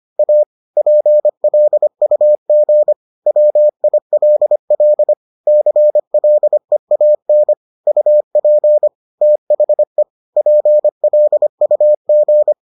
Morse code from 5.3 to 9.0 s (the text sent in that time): CLEAN UP